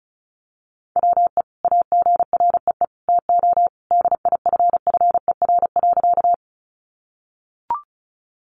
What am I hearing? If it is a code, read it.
Morse code, heard as a keyed tone, about 34 wpm, 700 Hz. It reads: WE AGREE TO DIFFER.
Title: Sentences from Top 1100 Words 35wpm